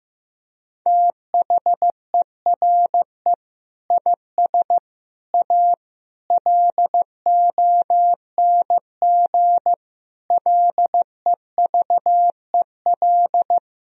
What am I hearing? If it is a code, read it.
Morse code, heard as a keyed tone, about 16 wpm, 700 Hz. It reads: THERE IS A LONG LEVEL
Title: Sentences from Top 1000 Words - Review Part 4 - 15wpm